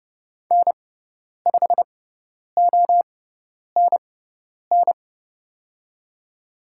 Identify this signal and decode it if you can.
Morse code, heard as a keyed tone, about 30 wpm, spaced as for 10 wpm, 700 Hz. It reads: N5ONN
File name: Lesson 16g - Letter c - Call Signs 30wpm